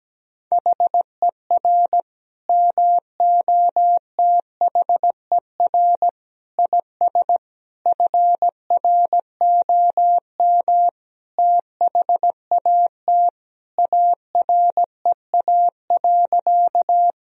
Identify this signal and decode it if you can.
Morse code, heard as a keyed tone, about 17 wpm, 700 Hz. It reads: HER MOTHER IS FROM THAT AREA.
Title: Sentences from Top 1000 Words - Review Part 2 - 17wpm